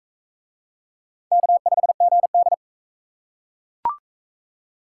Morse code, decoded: K5GD

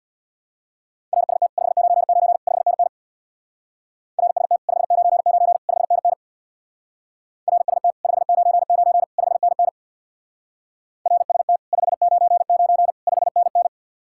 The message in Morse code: RST 599 5NN RST 599 5NN RST 599 5NN RST 599 5NN

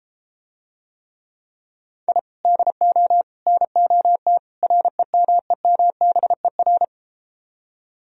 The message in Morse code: I DO NOT REMEMBER

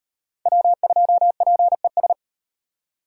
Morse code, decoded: W2PES

